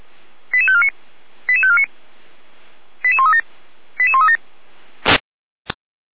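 Five-tone selective call (ZVEI), heard as quick runs of five tones, tones of about 70 ms. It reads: 88549 88549 88137 88137
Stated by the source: ZVEI five-tone selective call received on an SDR, from the Feuerwehr Orlamünde fire brigade audio server.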